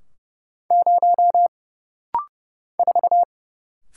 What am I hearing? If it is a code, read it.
Morse code, heard as a keyed tone, about 30 wpm, 700 Hz. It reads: 0 4
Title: Single Letter-Number Alt - Rapid-Fire 30wpm